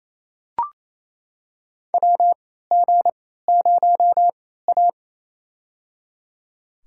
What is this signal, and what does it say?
Morse code, heard as a keyed tone, about 28 wpm, spaced as for 16 wpm, 700 Hz. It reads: WG0A
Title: USA Call Signs - 4 Characters - 3x Character Spacing - 28wpm